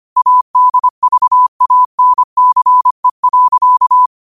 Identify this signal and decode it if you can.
Morse code, heard as a keyed tone, about 25 wpm, 1000 Hz. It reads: ADVANCE.
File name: qotdfiles / 25wpm